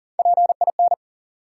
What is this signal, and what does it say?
Morse code, heard as a keyed tone, about 40 wpm, 700 Hz. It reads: PIN
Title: Three Letter Words - 40wpm